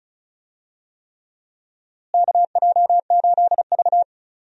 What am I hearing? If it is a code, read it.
Morse code, heard as a keyed tone, about 34 wpm, 700 Hz. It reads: KJ8V
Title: USA Call Signs - 4 Characters - Repeated 3x - 35wpm